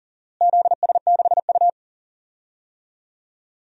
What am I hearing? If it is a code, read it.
Morse code, heard as a keyed tone, about 40 wpm, 700 Hz. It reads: ZS6U
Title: DXCC-100 Call Signs - 4 Characters - Repeated 2x - 40wpm